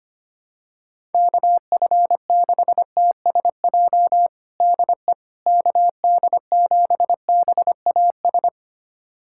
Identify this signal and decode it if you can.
Morse code, heard as a keyed tone, about 25 wpm, 700 Hz. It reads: KF6TSJ DE KD7BAS
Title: USA Callsign DE Callsign - 6 Characters - 25wpm